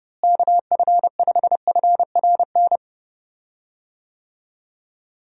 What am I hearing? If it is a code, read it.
Morse code, heard as a keyed tone, about 30 wpm, 700 Hz. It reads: KF5FRN
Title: USA Call Signs - 6 Characters - Repeated 2x - 30wpm